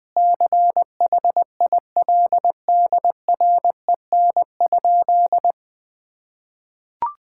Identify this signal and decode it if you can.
Morse code, heard as a keyed tone, about 20 wpm, 700 Hz. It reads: CHILDREN?
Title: Sentences from Top 1000 Words - Review Part 5 - 20wpm